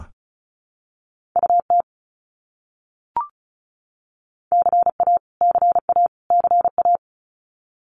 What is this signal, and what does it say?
Morse code, heard as a keyed tone, about 34 wpm, 700 Hz. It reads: UT CA CA CA
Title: US States Repeated 3x 35wpm